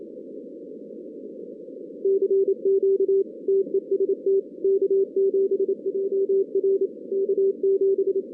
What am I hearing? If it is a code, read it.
Morse code, heard as a keyed tone, about 27 wpm, 390 Hz. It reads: CQ TEST K7JR K7
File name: 2016 arrl dx cw / 2016-ARRL-DX-CW-K5TR- - 0-39